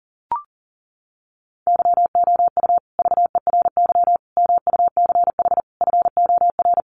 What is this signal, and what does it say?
Morse code, heard as a keyed tone, about 40 wpm, 700 Hz. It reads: YOU VERY MUCH FOR
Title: Sets of 5 Words Encore 40wpm